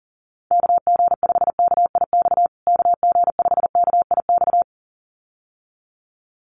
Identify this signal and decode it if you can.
Morse code, heard as a keyed tone, about 40 wpm, 700 Hz. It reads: KG5KIX KG5KIX
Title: USA Call Signs Repeated 2x 40wpm